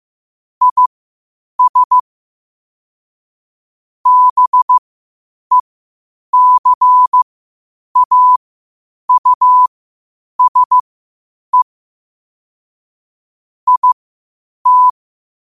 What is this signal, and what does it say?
Morse code, heard as a keyed tone, about 15 wpm, spaced as for 8 wpm, 1000 Hz. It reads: IS BECAUSE IT
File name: qotdfiles / 8wpm